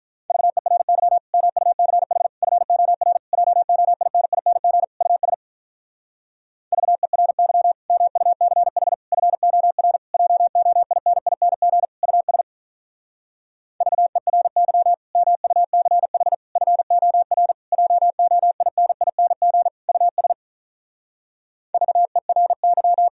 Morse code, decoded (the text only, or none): VERY MUCH FOR JOINING US VERY MUCH FOR JOINING US VERY MUCH FOR JOINING US VERY